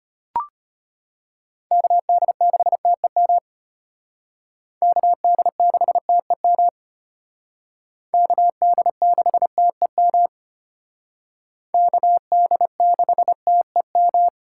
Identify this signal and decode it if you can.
Morse code, heard as a keyed tone, about 33 wpm, 700 Hz. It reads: KD6TEM KD6TEM KD6TEM KD6TEM